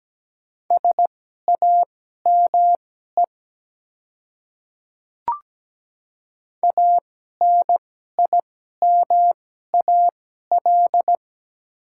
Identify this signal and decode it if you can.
Morse code, heard as a keyed tone, about 17 wpm, spaced as for 12 wpm, 700 Hz. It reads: SAME ANIMAL
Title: 2x Character Spacing - Top 300 Words - 17wpm